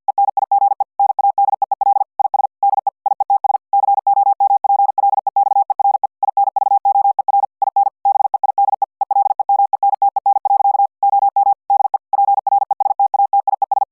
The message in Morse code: ENIGE NADEEL IS DE IETS COMPLEXERE INVOER IN BEIDE ELEMENTEN. OM DE PRESTATIES